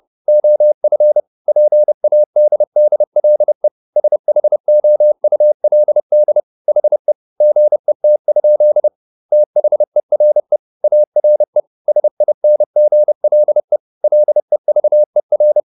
Morse code, heard as a keyed tone, about 30 wpm, 600 Hz. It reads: OF PADDLE SHOULD HE GET? THERE ARE SINGLE LEVER
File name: SS301 30